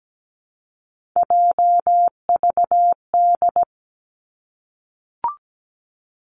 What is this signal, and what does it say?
Morse code, heard as a keyed tone, about 17 wpm, 700 Hz. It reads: JVD
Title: Three Challenging Letter-Number Combinations 17wpm